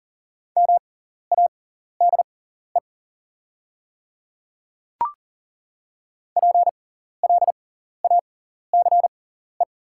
Morse code, heard as a keyed tone, about 42 wpm, spaced as for 14 wpm, 690 Hz. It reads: MADE PLACE
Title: 6x Character Spacing - Top 200 Words - 40wpm